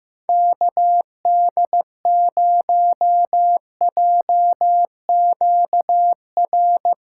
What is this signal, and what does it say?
Morse code, heard as a keyed tone, about 15 wpm, 700 Hz. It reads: KD0JQR
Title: Speed Racing - USA 2x3 Call Signs 15wpm